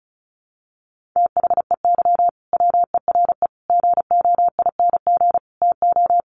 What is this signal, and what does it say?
Morse code, heard as a keyed tone, about 34 wpm, 700 Hz. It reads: THEY WERE GOING TO
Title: Sets of 4 Words 35wpm